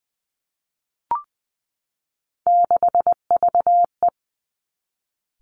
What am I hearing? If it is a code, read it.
Morse code, heard as a keyed tone, about 20 wpm, 700 Hz. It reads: 6VE
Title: Three Challenging Letter-Number Combinations 20wpm